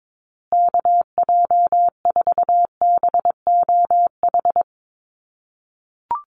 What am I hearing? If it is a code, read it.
Morse code, heard as a keyed tone, about 22 wpm, 700 Hz. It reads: KJ4BOH
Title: USA Call Signs Repeated 2x 22wpm